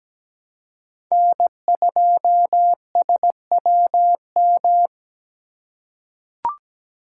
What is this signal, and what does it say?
Morse code, heard as a keyed tone, about 17 wpm, 700 Hz. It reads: N2SWM
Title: USA Call Signs - 5 Characters - 2x Character Spacing - 17wpm